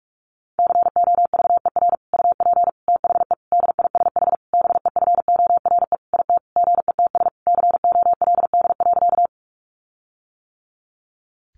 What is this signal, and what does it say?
Morse code, heard as a keyed tone, about 46 wpm, 700 Hz. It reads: COVER UP THE DISH BEFORE IT GETS COLD.
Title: Sentences from Top 1300 Words 45wpm